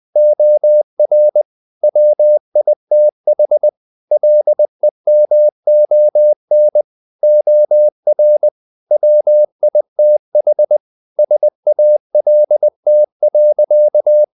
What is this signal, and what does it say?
Morse code, heard as a keyed tone, about 20 wpm, 600 Hz. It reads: OR WITH LEMON OR WITH SALT.
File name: the old man and the sea / the old man and the sea part 1 20wpm